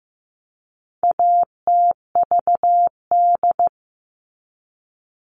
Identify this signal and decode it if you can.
Morse code, heard as a keyed tone, about 15 wpm, 700 Hz. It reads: ATVD
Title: All Cut Numbers - 4-Digit 15wpm